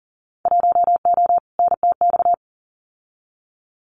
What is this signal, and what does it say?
Morse code, heard as a keyed tone, about 40 wpm, 700 Hz. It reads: 1O NTX
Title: Winter Field Day Exchanges 40wpm